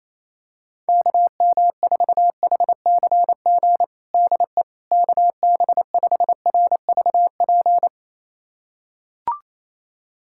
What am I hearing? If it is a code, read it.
Morse code, heard as a keyed tone, about 28 wpm, 700 Hz. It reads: KM4HCG DE KB5RVP
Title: USA Callsign DE Callsign 28wpm